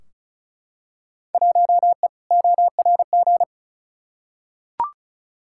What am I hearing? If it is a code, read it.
Morse code, heard as a keyed tone, about 36 wpm, 700 Hz. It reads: 1E ORG
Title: Field Day Exchanges - 35wpm